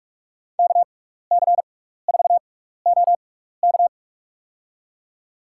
Morse code, heard as a keyed tone, about 47 wpm, spaced as for 16 wpm, 700 Hz. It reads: KC4OK
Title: USA Call Signs - 5 Characters - 6x Character Spacing - 45wpm